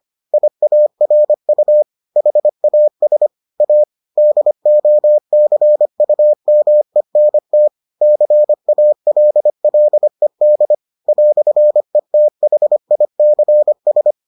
Morse code, heard as a keyed tone, about 25 wpm, 600 Hz. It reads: IARU HAS A DOCUMENT CALLED "ETHICS